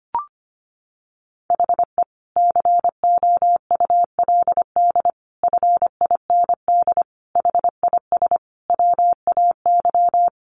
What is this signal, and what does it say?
Morse code, heard as a keyed tone, about 25 wpm, 700 Hz. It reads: HE COULD FIND HIS WAY
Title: Sentences from Top 1400 Words 25wpm